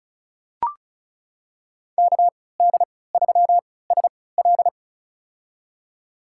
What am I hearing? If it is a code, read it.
Morse code, heard as a keyed tone, about 34 wpm, spaced as for 20 wpm, 700 Hz. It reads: KD3SL
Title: USA Call Signs - 3x Character Spacing - 35wpm